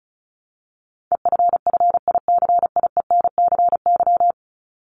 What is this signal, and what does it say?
Morse code, heard as a keyed tone, about 34 wpm, 700 Hz. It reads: EFFICIENCY
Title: Top 1501-1750 Words 35wpm